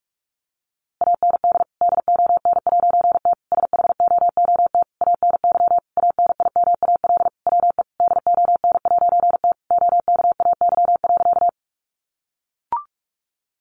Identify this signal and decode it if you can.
Morse code, heard as a keyed tone, about 44 wpm, 700 Hz. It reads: AND DON'T SHOOT ANY ANIMAL WE DON'T OKAY.
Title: A Sound Of Thunder 45wpm